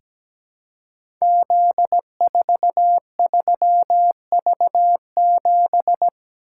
Morse code, heard as a keyed tone, about 17 wpm, 700 Hz. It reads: Z43V7
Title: Code Groups - 5 Letter:Numbers - 17wpm